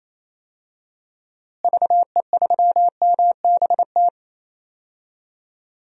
Morse code, heard as a keyed tone, about 28 wpm, 700 Hz. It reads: VE3MBT
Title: DXCC-100 Call Signs - 6 Characters - Repeated 3x - 28wpm